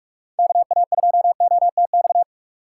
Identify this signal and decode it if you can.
Morse code, heard as a keyed tone, about 46 wpm, 700 Hz. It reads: KA2OTX
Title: USA Call Signs - 6 Characters - 6x Character Spacing - 45wpm